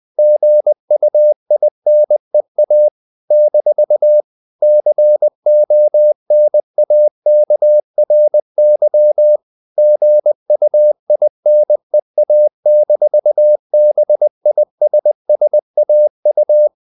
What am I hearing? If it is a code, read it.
Morse code, heard as a keyed tone, about 20 wpm, 600 Hz. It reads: GUINEA - CONAKRY GUINEA-BISSAU
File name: CW / Practice / NationalCapitals20